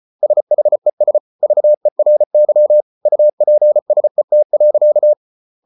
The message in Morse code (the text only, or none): SHES VERY UPSET.